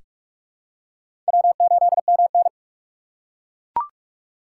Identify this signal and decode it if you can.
Morse code, heard as a keyed tone, about 42 wpm, 700 Hz. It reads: W8MN
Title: USA Call Signs - 4 Characters - 3x Character Spacing - 45wpm